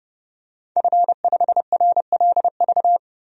Morse code, heard as a keyed tone, about 30 wpm, 700 Hz. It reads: F5RLV
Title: DXCC-100 Call Signs - 5 Characters - 30wpm